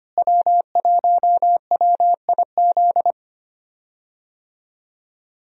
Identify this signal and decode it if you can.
Morse code, heard as a keyed tone, about 25 wpm, 700 Hz. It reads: W1WIZ
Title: USA Call Signs - 5 Characters - Repeated 3x - 25wpm